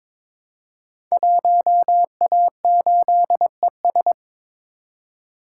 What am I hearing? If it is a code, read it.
Morse code, heard as a keyed tone, about 22 wpm, 700 Hz. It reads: 1A8ES
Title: Code Groups - 5 Letter:Numbers - 22wpm